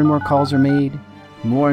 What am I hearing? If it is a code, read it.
DTMF, heard as telephone touch tones, tones of about 110 ms, with speech. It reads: 936